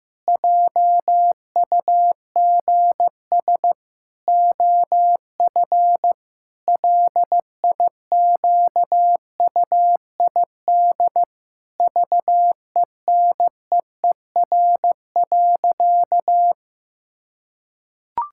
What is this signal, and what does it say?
Morse code, heard as a keyed tone, about 15 wpm, 700 Hz. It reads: JUGS OF LIQUID VENEER.
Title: Pangram Challenge v2 - 15wpm